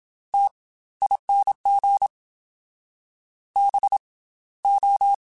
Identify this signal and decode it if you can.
Morse code, heard as a keyed tone, about 27 wpm, 800 Hz. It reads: T ING B O
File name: Special INGs / Special INGs-mp3s / Slow ing @25